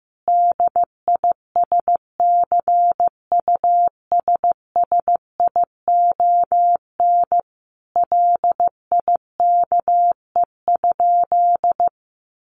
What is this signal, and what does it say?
Morse code, heard as a keyed tone, about 14 wpm, 700 Hz. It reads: DISCUSSION LIKE?
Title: Sentences from Top 1500 Words 15wpm